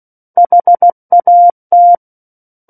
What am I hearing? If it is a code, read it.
Morse code, heard as a keyed tone, about 16 wpm, 700 Hz. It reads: HAT
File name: Book One / 07-Chapter 7 - 16 WPM